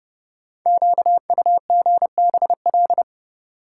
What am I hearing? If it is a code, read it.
Morse code, heard as a keyed tone, about 30 wpm, 700 Hz. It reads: QUGBL
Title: Code Groups - 5 Letters - 30wpm